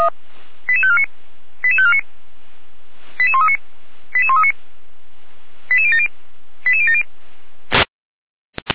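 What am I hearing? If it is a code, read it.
Five-tone selective call (ZVEI), heard as quick runs of five tones, tones of about 70 ms. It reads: 88549 88549 88139 88139 88070 88070
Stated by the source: ZVEI five-tone call 88549 received on SDR, archived by feuerwehr-orlamuende.org fire brigade.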